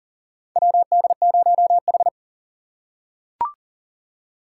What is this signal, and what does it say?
Morse code, heard as a keyed tone, about 40 wpm, 700 Hz. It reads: WD0H
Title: USA Call Signs - 4 Characters - 6x Character Spacing - 40wpm